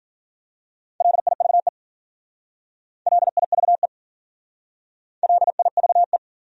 Morse code, heard as a keyed tone, about 47 wpm, 700 Hz. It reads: LIVE LIVE LIVE
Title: Speed-Racing Top 500 Words 40wpm